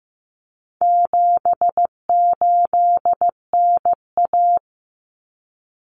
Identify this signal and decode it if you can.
Morse code, heard as a keyed tone, about 15 wpm, 700 Hz. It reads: 78NA
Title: Common Cut Numbers - 4-Digit 15wpm